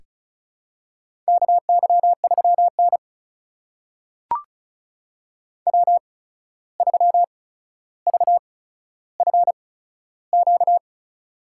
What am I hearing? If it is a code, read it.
Morse code, heard as a keyed tone, about 36 wpm, spaced as for 14 wpm, 700 Hz. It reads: KY3N W3VFQ
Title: USA Call Signs - 8x Character Spacing - 35wpm